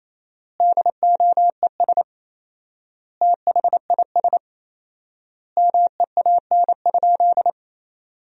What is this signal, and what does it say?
Morse code, heard as a keyed tone, about 28 wpm, 700 Hz. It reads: DOES THIS MEAN?
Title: 4x Word-Spacing - Sentences from Top 200 Words - 28wpm